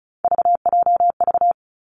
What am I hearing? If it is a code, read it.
Morse code, heard as a keyed tone, about 36 wpm, 700 Hz. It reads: VJV
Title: Three Challenging Letter-Number Combinations 35wpm